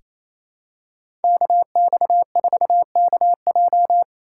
Morse code, heard as a keyed tone, about 28 wpm, 700 Hz. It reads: KX4KJ
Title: USA Call Signs - 5 Characters - 6x Character Spacing - 28wpm